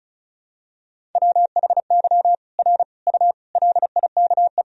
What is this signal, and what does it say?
Morse code, heard as a keyed tone, about 34 wpm, 700 Hz. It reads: WHY R U LIKE
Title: Sentences from Top 200 Words and Abbreviations 35wpm